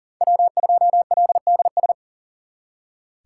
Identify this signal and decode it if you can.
Morse code, heard as a keyed tone, about 52 wpm, 700 Hz. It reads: W2LDS